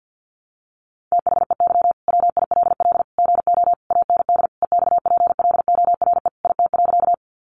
Morse code, heard as a keyed tone, about 50 wpm, 700 Hz. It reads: THEY WILL GO AND EXPLORE IT.
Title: Sentences from Top 1100 Words 50wpm